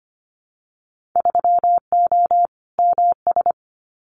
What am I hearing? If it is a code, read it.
Morse code, heard as a keyed tone, about 25 wpm, 700 Hz. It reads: 3O MS
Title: Winter Field Day Exchanges Encore 25wpm